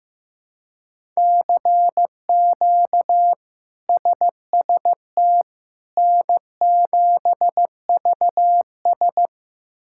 Morse code, heard as a keyed tone, about 16 wpm, 700 Hz. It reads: CQ SST N7VS
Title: K1USN SST QSO Practice - 15wpm